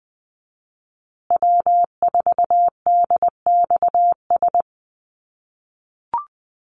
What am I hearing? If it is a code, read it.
Morse code, heard as a keyed tone, about 20 wpm, 700 Hz. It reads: W4DXS
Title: USA Call Signs Repeated 2x 20wpm